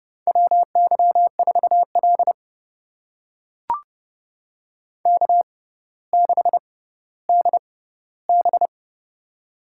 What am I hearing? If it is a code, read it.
Morse code, heard as a keyed tone, about 30 wpm, spaced as for 15 wpm, 700 Hz. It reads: WY4L K6DB
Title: USA Call Signs - 4 Characters - 6x Character Spacing - 30wpm